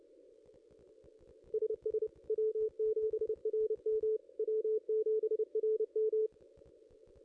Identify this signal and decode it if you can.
Morse code, heard as a keyed tone, about 28 wpm, 440 Hz. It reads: SS W7RM W7RM
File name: contests / ss-cw-k5tr-op-n5rz -0500